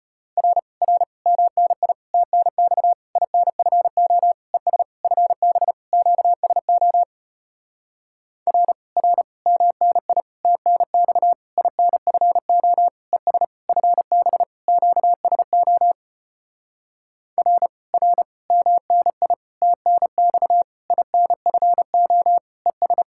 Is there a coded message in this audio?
Yes: Morse code from 0.4 to 23.0 s, heard as a keyed tone, about 33 wpm, 700 Hz. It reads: R R MNI TNX INFO ES FB QSO R R MNI TNX INFO ES FB QSO R R MNI TNX INFO ES